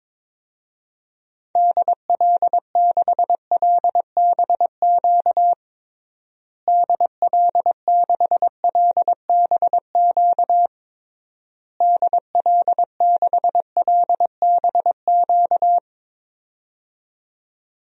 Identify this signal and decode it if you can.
Morse code, heard as a keyed tone, about 22 wpm, 700 Hz. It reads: DL6LBQ DL6LBQ DL6LBQ
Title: DXCC-100 Call Signs - 6 Characters - Repeated 3x - 22wpm